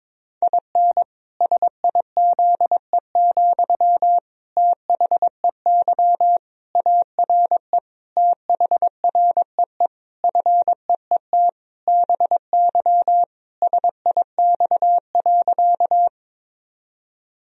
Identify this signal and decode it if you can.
Morse code, heard as a keyed tone, about 21 wpm, 700 Hz. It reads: IN SIZE, THEY ARE THREE FEET BY SIX.
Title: Sentences from Top 1000 Words - Review Part 2 - 22wpm